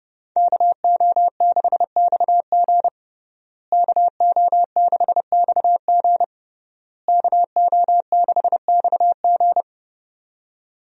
Morse code, heard as a keyed tone, about 32 wpm, 700 Hz. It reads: KO6XG KO6XG KO6XG
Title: USA Call Signs - 5 Characters - Repeated 3x - 30wpm